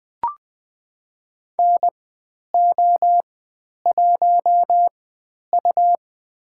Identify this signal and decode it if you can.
Morse code, heard as a keyed tone, about 20 wpm, spaced as for 10 wpm, 700 Hz. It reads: NO1U
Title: 102 Lesson 17g - Letter u - Call Signs 20wpm